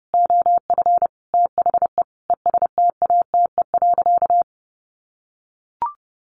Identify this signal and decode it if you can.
Morse code, heard as a keyed tone, about 30 wpm, 700 Hz. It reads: OF THE ESTATE.
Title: Sentences from Top 1200 Words 30wpm